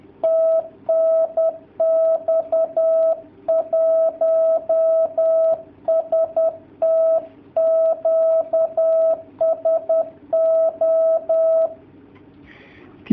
Morse code, heard as a keyed tone, about 10 wpm, 660 Hz. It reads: TNX 1ST QSO
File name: jh8cbh / 2019cw / sp10- w